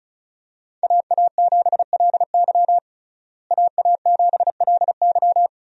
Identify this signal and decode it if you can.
Morse code, heard as a keyed tone, about 36 wpm, 700 Hz. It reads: AA7LY AA7LY
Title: USA Call Signs - 5 Characters - Repeated 3x - 35wpm